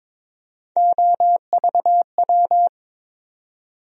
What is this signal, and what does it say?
Morse code, heard as a keyed tone, about 22 wpm, 700 Hz. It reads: OVW
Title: USA Call Sign Postfixes - 3 Letters 22wpm